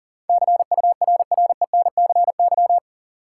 Morse code, heard as a keyed tone, about 40 wpm, 700 Hz. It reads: CURRENCY
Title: Sentences from Top 1500 Words - New Word Drill - 40wpm